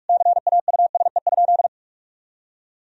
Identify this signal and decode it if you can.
Morse code, heard as a keyed tone, about 46 wpm, 700 Hz. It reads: CAUSE?
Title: Sentences from Top 1000 Words - Review Part 3 - 45wpm